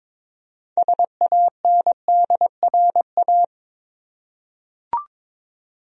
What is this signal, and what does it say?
Morse code, heard as a keyed tone, about 22 wpm, 700 Hz. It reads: SANDRA
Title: USA Common Names - 22wpm